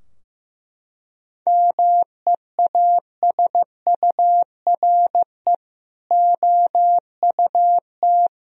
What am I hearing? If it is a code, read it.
Morse code, heard as a keyed tone, about 16 wpm, 700 Hz. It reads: MEASURE OUT
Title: Sentences from Top 1000 Words - Review Part 2 - 15wpm